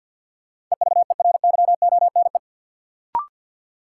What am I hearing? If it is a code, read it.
Morse code, heard as a keyed tone, about 48 wpm, 700 Hz. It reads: EVERYONE
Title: Sentences from Top 800 Words - New Word Drill - 50wpm